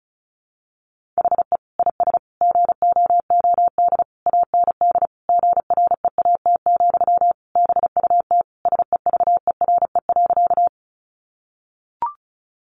Morse code, heard as a keyed tone, about 34 wpm, 700 Hz. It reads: HE IS GOOD AND GREAT, BUT SEVERE.
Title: Sentences from Top 1400 Words 35wpm